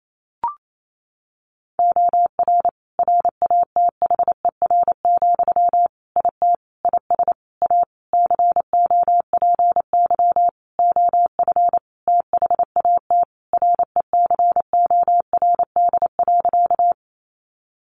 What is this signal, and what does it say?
Morse code, heard as a keyed tone, about 28 wpm, 700 Hz. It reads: OR RATHER, IT IS A COPY OF THAT RECORD.
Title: Sentences from Top 1400 Words 28wpm